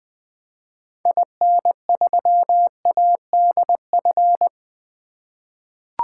Morse code, heard as a keyed tone, about 20 wpm, 700 Hz. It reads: IN3ADF
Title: 2023 SOTA Activator Call Signs - 20wpm